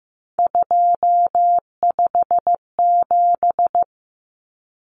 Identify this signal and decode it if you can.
Morse code, heard as a keyed tone, about 15 wpm, 700 Hz. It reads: 257
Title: Common Cut Numbers - 3-Digit 15wpm